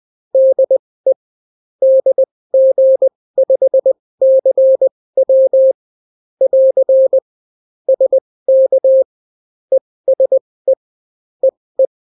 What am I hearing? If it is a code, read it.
Morse code, heard as a keyed tone, about 20 wpm, spaced as for 16 wpm, 530 Hz. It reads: DE DG5CW <AR> SK ESE EE